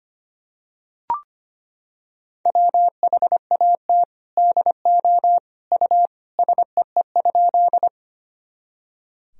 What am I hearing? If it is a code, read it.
Morse code, heard as a keyed tone, about 25 wpm, 700 Hz. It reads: WHAT DO U SEE?
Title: Sentences from Top 100 Words and Abbreviations 25wpm